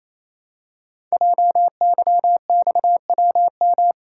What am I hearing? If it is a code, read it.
Morse code, heard as a keyed tone, about 29 wpm, 700 Hz. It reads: JYXWM
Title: Code Groups - 5 Letters - 28wpm